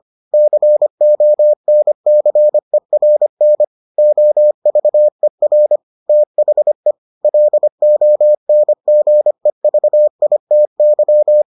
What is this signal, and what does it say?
Morse code, heard as a keyed tone, about 25 wpm, 600 Hz. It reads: CONCERN OVER THE LONGEVITY